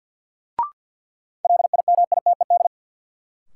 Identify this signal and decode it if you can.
Morse code, heard as a keyed tone, about 53 wpm, 700 Hz. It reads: LIMITED
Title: Sentences from Top 900 Words - New Word Drill - 50wpm